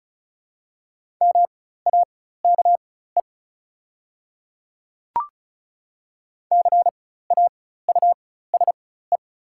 Morse code, heard as a keyed tone, about 34 wpm, spaced as for 16 wpm, 700 Hz. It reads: MAKE CAUSE
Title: 4x Character Spacing - Top 200 Words - 35wpm